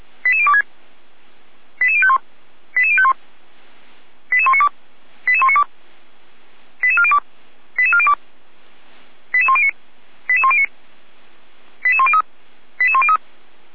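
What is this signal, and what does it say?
Five-tone selective call (ZVEI), heard as quick runs of five tones, tones of about 70 ms. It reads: 88026 88051 88051 88192 88192 88402 88402 88109 88109 88103 88103
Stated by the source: Five-tone ZVEI selective call received on SDR, fire brigade Feuerwehr Orlamünde.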